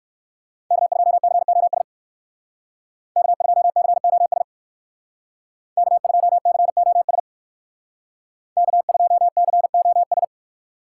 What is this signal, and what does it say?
Morse code, heard as a keyed tone, about 58 wpm, 700 Hz. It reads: K2COS K2COS K2COS K2COS